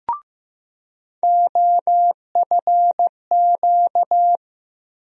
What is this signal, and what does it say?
Morse code, heard as a keyed tone, about 15 wpm, 700 Hz. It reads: OFQ
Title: USA Call Sign Postfixes - 3 Letters 15wpm